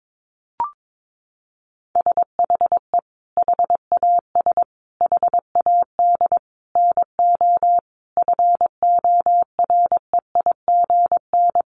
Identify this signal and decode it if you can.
Morse code, heard as a keyed tone, about 22 wpm, 700 Hz. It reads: SHE HAS HAD NO FOREIGN
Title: Sentences from Top 1400 Words 22wpm